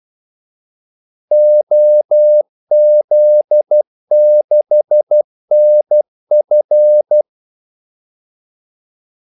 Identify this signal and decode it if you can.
Morse code, heard as a keyed tone, about 12 wpm, 600 Hz. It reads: OZ6NF
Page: 2 ekstra ordmellemrum mellem kaldesignalerne